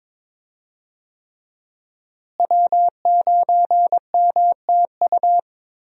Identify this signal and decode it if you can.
Morse code, heard as a keyed tone, about 21 wpm, 700 Hz. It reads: W9MTU
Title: USA Call Signs - 5 Characters - Repeated 3x - 22wpm